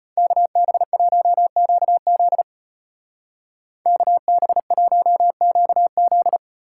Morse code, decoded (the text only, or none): KB1QZ KB1QZ